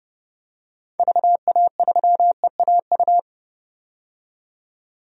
Morse code, heard as a keyed tone, about 32 wpm, 700 Hz. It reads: VA3EAU
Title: DXCC-100 Call Signs - 6 Characters - 30wpm